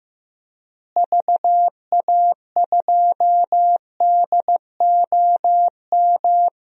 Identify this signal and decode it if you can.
Morse code, heard as a keyed tone, about 15 wpm, 700 Hz. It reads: VA2DOM
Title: DXCC-100 Call Signs - 6 Characters - Repeated 3x - 15wpm